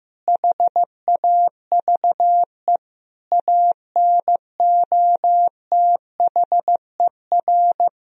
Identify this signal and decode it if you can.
Morse code, heard as a keyed tone, about 16 wpm, 700 Hz. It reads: HAVE ANOTHER
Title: Sentences from Top 1000 Words - Review Part 2 - 15wpm